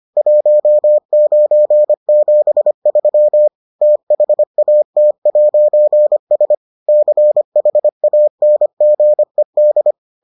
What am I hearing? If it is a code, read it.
Morse code, heard as a keyed tone, about 25 wpm, 600 Hz. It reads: 1973 THAT'S CHANGED